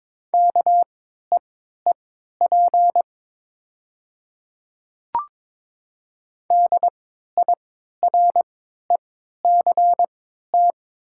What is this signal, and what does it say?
Morse code, heard as a keyed tone, about 22 wpm, spaced as for 12 wpm, 690 Hz. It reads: KEEP DIRECT
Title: 3x Character Spacing - Top 500 Words - 22wpm